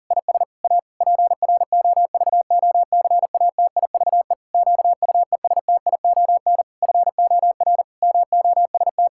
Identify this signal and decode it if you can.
Morse code, heard as a keyed tone, about 40 wpm, 700 Hz. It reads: IS A PROVOCATIVE QUESTION FOR MOST
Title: Pangram Challenge v2 - 40wpm